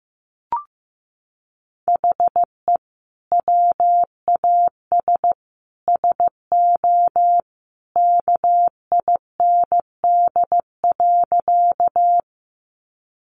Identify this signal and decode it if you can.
Morse code, heard as a keyed tone, about 15 wpm, 700 Hz. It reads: HE WAS SO KIND.
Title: Sentences from Top 200 Words 15wpm